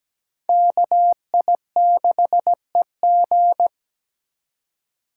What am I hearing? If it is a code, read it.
Morse code, heard as a keyed tone, about 17 wpm, 700 Hz. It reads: KI6EG
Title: USA Call Signs - 5 Characters - 4x Character Spacing - 17wpm